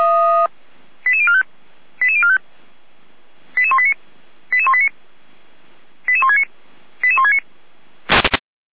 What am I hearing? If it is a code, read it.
Five-tone selective call (ZVEI), heard as quick runs of five tones, tones of about 70 ms. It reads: 88045 88045 88189 88189 88169 88169
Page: /455/SDR/ZVEI/